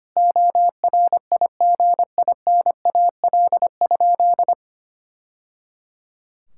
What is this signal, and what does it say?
Morse code, heard as a keyed tone, about 25 wpm, 700 Hz. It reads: ORIGINAL?
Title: Sentences from Top 1000 Words - Review Part 6 - 25wpm